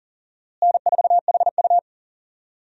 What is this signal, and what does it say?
Morse code, heard as a keyed tone, about 38 wpm, 700 Hz. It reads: N4HU
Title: USA Call Signs - 4 Characters - Repeated 3x - 40wpm